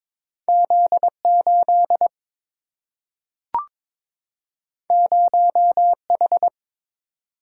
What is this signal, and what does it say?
Morse code, heard as a keyed tone, about 22 wpm, 700 Hz. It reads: Z 8 0 H
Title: Single-Letter Single-Number 22wpm